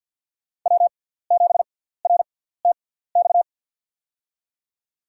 Morse code, heard as a keyed tone, about 53 wpm, spaced as for 17 wpm, 700 Hz. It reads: W7RTX
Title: USA Call Signs - 5 Characters - 6x Character Spacing - 50wpm